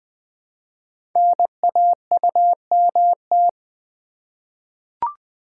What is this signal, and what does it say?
Morse code, heard as a keyed tone, about 20 wpm, 700 Hz. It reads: NAUMT
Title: Code Groups - 5 Letters - 20wpm